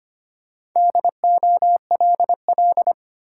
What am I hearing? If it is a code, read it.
Morse code, heard as a keyed tone, about 25 wpm, 700 Hz. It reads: DOLL
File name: practice / 25wpm / Easy Words with Hints - 25wpm